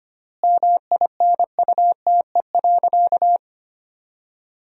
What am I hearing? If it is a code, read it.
Morse code, heard as a keyed tone, about 25 wpm, 700 Hz. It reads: MINUTE.
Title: 4x Word-Spacing - Sentences from Top 500 Words - 25wpm